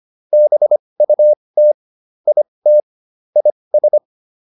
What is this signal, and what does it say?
Morse code, heard as a keyed tone, about 25 wpm, spaced as for 20 wpm, 600 Hz. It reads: BUT IT IS
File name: WD202 20